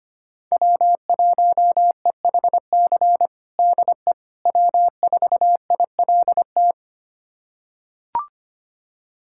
Morse code, heard as a keyed tone, about 25 wpm, 700 Hz. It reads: W1EHC DE W4ILT
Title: USA Callsign DE Callsign - 5 Characters - 25wpm